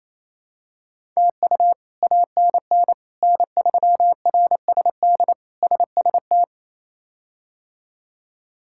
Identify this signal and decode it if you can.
Morse code, heard as a keyed tone, about 28 wpm, 700 Hz. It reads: TU ANN N3RSD SST
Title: K1USN SST QSO Practice - 28wpm